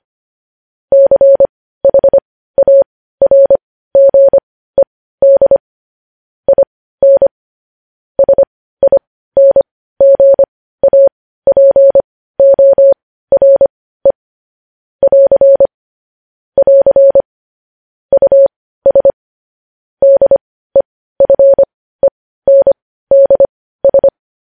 Morse code, heard as a keyed tone, about 25 wpm, spaced as for 15 wpm, 550 Hz. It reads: CHARGED IN SINGAPORE <AR> <AR> US DEFENDS
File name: podcast / News headlines-1x-25x15